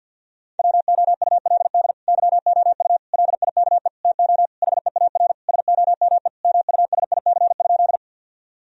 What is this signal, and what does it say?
Morse code, heard as a keyed tone, about 51 wpm, 700 Hz. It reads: WOULD YOU LIKE TO HEAR SOME MUSIC?
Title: Sentences from Top 1000 Words - Review Part 1 - 50wpm